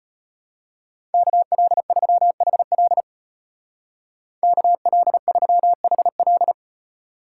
Morse code, heard as a keyed tone, about 31 wpm, 700 Hz. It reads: KL3HL KL3HL
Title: Speed Racing - USA 2x2 Call Signs 25wpm